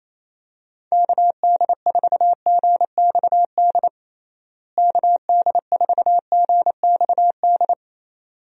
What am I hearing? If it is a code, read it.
Morse code, heard as a keyed tone, about 28 wpm, 700 Hz. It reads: KD4GXD KD4GXD
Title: USA Call Signs - 6 Characters - Repeated 2x - 28wpm